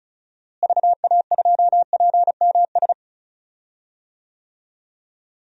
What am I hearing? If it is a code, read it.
Morse code, heard as a keyed tone, about 34 wpm, 700 Hz. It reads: VA2PMS
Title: DXCC-100 Call Signs - 6 Characters - Repeated 2x - 35wpm